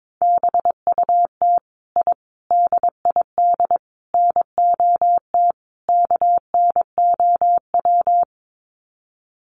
Morse code, heard as a keyed tone, about 22 wpm, 700 Hz. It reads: BUT I DID NOT KNOW
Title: Sets of 5 Words Encore 22wpm